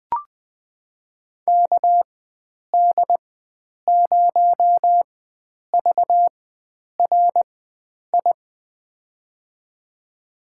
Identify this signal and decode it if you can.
Morse code, heard as a keyed tone, about 21 wpm, spaced as for 9 wpm, 700 Hz. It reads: KD0VRI
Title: USA Call Signs - 6 Characters - 4x Character Spacing - 20wpm